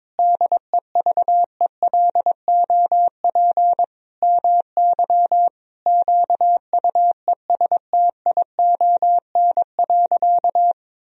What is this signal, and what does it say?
Morse code, heard as a keyed tone, about 22 wpm, 700 Hz. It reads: DEVELOP MY QUESTION.
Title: Sentences from Top 1000 Words - Review Part 2 - 22wpm